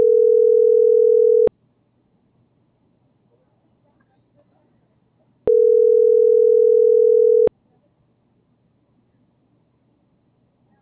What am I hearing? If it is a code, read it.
Ringback tone (North America), 440 + 480 Hz.